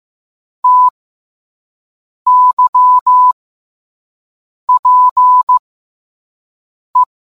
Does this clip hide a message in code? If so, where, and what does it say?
Yes: Morse code from 0.6 to 7.0 s, heard as a keyed tone, about 15 wpm, spaced as for 5 wpm, 1000 Hz. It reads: TYPE